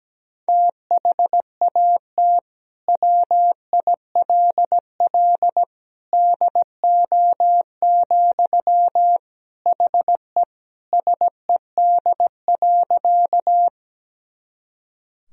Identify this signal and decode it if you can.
Morse code, heard as a keyed tone, about 17 wpm, 700 Hz. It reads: THAT WILL DO, HE SED.
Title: Sentences from Top 100 Words and Abbreviations 17wpm